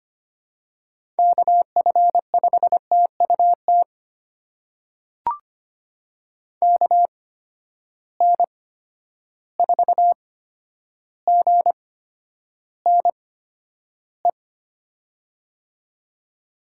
Morse code, heard as a keyed tone, about 25 wpm, spaced as for 11 wpm, 700 Hz. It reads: KF5TUT KN4GNE
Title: USA Call Signs - 8x Character Spacing - 25wpm